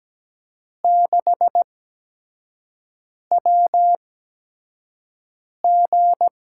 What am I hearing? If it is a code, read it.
Morse code, heard as a keyed tone, about 17 wpm, spaced as for 5 wpm, 700 Hz. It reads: 6WG
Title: USA Call Signs - 4 Characters - 8x Character Spacing - 17wpm